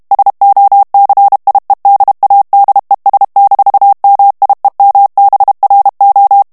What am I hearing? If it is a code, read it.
Morse code, heard as a keyed tone, about 32 wpm, 800 Hz. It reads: SOCIEDADES-MIEMBRO
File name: morse mp3 / ESPAGNOL TEXTE EN CLAIR 1 VITESSE 30 MOTS MINUTES